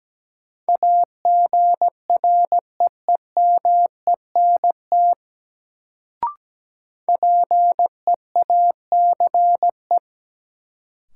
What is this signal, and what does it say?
Morse code, heard as a keyed tone, about 17 wpm, 700 Hz. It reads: AGREEMENT PEACE
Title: Sentences from Top 1500 Words - New Word Drill Alt - 17wpm